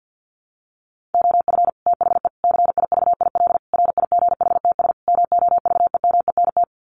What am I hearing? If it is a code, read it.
Morse code, heard as a keyed tone, about 50 wpm, 700 Hz. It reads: OF THE CIVIL RIGHTS MOVEMENT
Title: Sets of 5 Words Encore2 50wpm